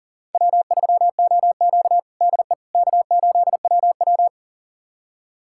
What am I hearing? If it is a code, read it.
Morse code, heard as a keyed tone, about 40 wpm, 700 Hz. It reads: W3OQ DE K8WW
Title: USA Callsign DE Callsign - 4 Characters - 40wpm